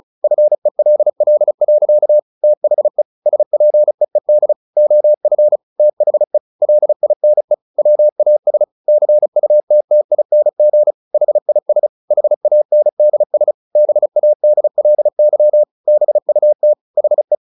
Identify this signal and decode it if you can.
Morse code, heard as a keyed tone, about 34 wpm, 600 Hz. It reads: FELL. THE SPEED OF THE LINE WAS CUTTING HIS HANDS BADLY BUT HE
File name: the old man and the sea 2 35wpm